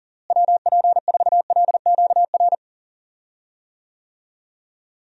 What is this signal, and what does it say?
Morse code, heard as a keyed tone, about 40 wpm, 700 Hz. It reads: WP4LQR
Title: USA Call Signs - 6 Characters - Repeated 3x - 40wpm